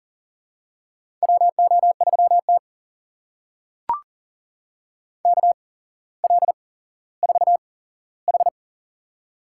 Morse code, heard as a keyed tone, about 38 wpm, spaced as for 17 wpm, 700 Hz. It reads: WO3T KL4H
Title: USA Call Signs - 4 Characters - 8x Character Spacing - 40wpm